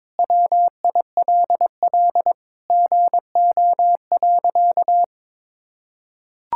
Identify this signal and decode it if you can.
Morse code, heard as a keyed tone, about 22 wpm, 700 Hz. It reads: WILL GO.
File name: practice / 22wpm / Sentences with 4-Word Phrases - 22wpm